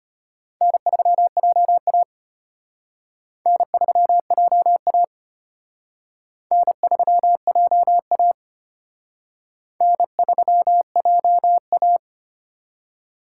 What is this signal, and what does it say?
Morse code, heard as a keyed tone, about 32 wpm, 700 Hz. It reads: N3JA N3JA N3JA N3JA